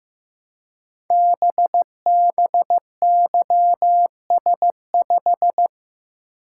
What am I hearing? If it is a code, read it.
Morse code, heard as a keyed tone, about 15 wpm, 700 Hz. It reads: BBYS5
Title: Code Groups - 5 Letter:Numbers - 15wpm